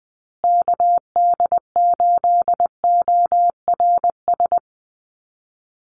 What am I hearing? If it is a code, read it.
Morse code, heard as a keyed tone, about 20 wpm, 700 Hz. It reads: KD8ORS
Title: USA Call Signs Repeated 2x 20wpm